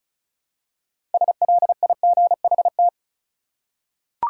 Morse code, heard as a keyed tone, about 36 wpm, 700 Hz. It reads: SLIGHT
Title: Six Letter Words - 35wpm